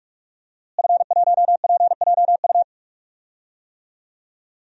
Morse code, heard as a keyed tone, about 42 wpm, 700 Hz. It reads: F1PJU
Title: DXCC-100 Call Signs - 5 Characters - Repeated 2x - 45wpm